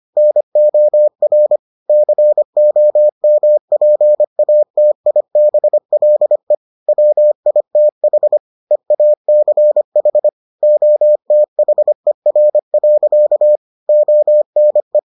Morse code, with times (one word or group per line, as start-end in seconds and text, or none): 0.2-1.6 NOR
1.9-6.5 COMPATIBLE
6.9-8.4 WITH
8.7-10.3 EACH
10.6-13.6 OTHER.
13.9-15.0 ONE